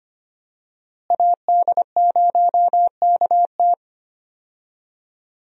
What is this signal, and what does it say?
Morse code, heard as a keyed tone, about 25 wpm, 700 Hz. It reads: AD0KT
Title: USA Call Signs - 5 Characters - Repeated 2x - 25wpm